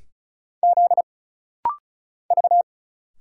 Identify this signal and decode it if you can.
Morse code, heard as a keyed tone, about 36 wpm, 700 Hz. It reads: Z V
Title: Single Letters - Rapid-Fire 35wpm